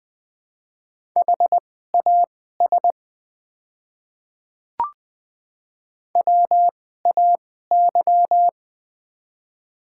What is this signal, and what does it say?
Morse code, heard as a keyed tone, about 20 wpm, spaced as for 15 wpm, 700 Hz. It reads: HAS WAY
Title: 2x Character Spacing - Top 200 Words - 20wpm